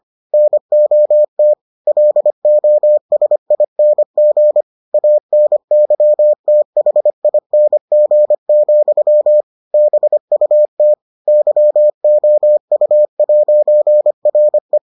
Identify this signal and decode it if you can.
Morse code, heard as a keyed tone, about 25 wpm, 600 Hz. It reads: NOT LOSING ANYTHING, BUT YOU'RE